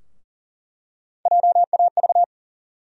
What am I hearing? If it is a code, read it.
Morse code, heard as a keyed tone, about 40 wpm, 700 Hz. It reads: JAV
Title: USA Call Sign Postfixes - 3 Letters 40wpm